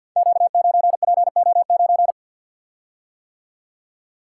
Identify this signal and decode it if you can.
Morse code, heard as a keyed tone, about 52 wpm, 700 Hz. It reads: Q9PO9
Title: Code Groups - 5 Letter:Numbers - 50wpm